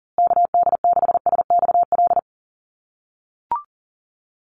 Morse code, heard as a keyed tone, about 40 wpm, 700 Hz. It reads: KD6SXL
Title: USA Call Signs Repeated 2x 40wpm